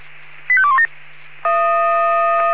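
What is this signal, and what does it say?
Five-tone selective call (ZVEI), heard as quick runs of five tones, tones of about 70 ms. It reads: 85217
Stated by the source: ZVEI five-tone selective call received on an SDR.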